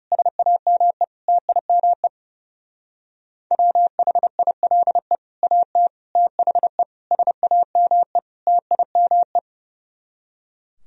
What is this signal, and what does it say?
Morse code, heard as a keyed tone, about 35 wpm, 700 Hz. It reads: SAME TIME WHILE AT THE SAME TIME
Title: Speed-Racing - Sets of 5 Words 30wpm